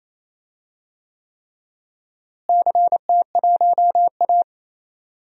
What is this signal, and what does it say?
Morse code, heard as a keyed tone, about 27 wpm, 700 Hz. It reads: CT1A
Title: DXCC-100 Call Signs - 4 Characters - Repeated 3x - 28wpm